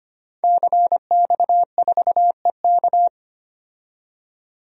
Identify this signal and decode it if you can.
Morse code, heard as a keyed tone, about 25 wpm, 700 Hz. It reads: CX4EK
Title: DXCC-100 Call Signs - 5 Characters - 25wpm